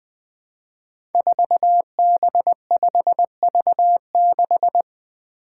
Morse code, heard as a keyed tone, about 20 wpm, 700 Hz. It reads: 4B5V6
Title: Five Challenging Letter-Number Combinations 20wpm